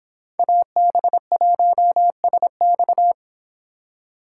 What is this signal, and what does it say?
Morse code, heard as a keyed tone, about 25 wpm, 700 Hz. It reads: AB1SX